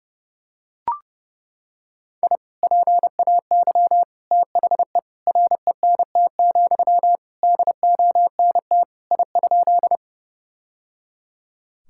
Morse code, heard as a keyed tone, about 31 wpm, 700 Hz. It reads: I PAY THE RENT, DONT I?
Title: Sentences from Top 1000 Words - Review Part 5 - 30wpm